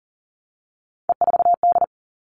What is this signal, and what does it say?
Morse code, heard as a keyed tone, about 40 wpm, 700 Hz. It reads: E4D
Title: All Cut Numbers - 3-Digit 40wpm